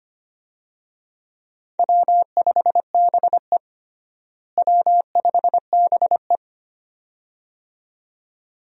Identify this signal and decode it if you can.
Morse code, heard as a keyed tone, about 25 wpm, 700 Hz. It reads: W5BE W5BE
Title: USA Call Signs - 4 Characters - Repeated 2x - 25wpm